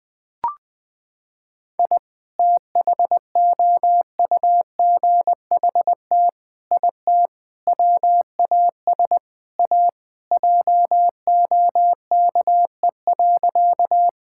Morse code, heard as a keyed tone, about 20 wpm, 700 Hz. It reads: I THOUGHT IT WAS A JOKE.
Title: Sentences with 5-Word Phrases - 20wpm